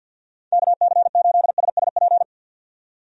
Morse code, heard as a keyed tone, about 51 wpm, 700 Hz. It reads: KC8SSP
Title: USA Call Signs - 6 Characters - 50wpm